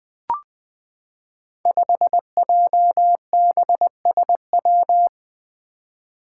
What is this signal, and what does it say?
Morse code, heard as a keyed tone, about 20 wpm, 700 Hz. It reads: 5JBSW
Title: Five Challenging Letter-Number Combinations 20wpm